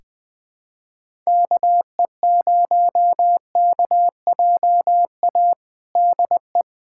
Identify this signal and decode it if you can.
Morse code, heard as a keyed tone, about 20 wpm, 700 Hz. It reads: KE0KJA DE
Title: USA Callsign DE Callsign - 6 Characters - 20wpm